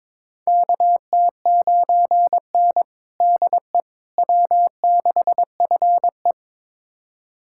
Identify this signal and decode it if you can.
Morse code, heard as a keyed tone, about 21 wpm, 700 Hz. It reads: KT9N DE W6FE
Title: USA Callsign DE Callsign - 4 Characters - 22wpm